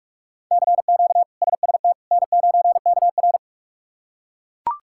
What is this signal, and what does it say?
Morse code, heard as a keyed tone, about 43 wpm, 700 Hz. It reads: CQ SST N9KR
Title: K1USN SST QSO Practice - 45wpm